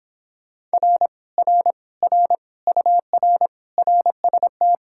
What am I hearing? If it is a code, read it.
Morse code, heard as a keyed tone, about 26 wpm, 700 Hz. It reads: R R R UR RST